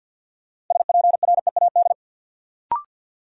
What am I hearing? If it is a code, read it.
Morse code, heard as a keyed tone, about 53 wpm, 700 Hz. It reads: SPREAD
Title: Sentences from Top 1200 Words - New Word Drill Alt - 50wpm